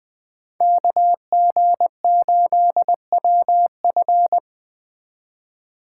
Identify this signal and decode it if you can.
Morse code, heard as a keyed tone, about 20 wpm, 700 Hz. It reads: KG8WF